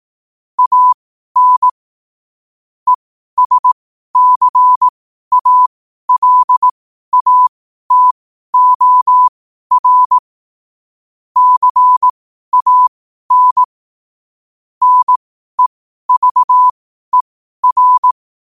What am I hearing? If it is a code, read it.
Morse code, heard as a keyed tone, about 18 wpm, spaced as for 12 wpm, 1000 Hz. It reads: AN ESCALATOR CAN NEVER
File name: qotdfiles / 12wpm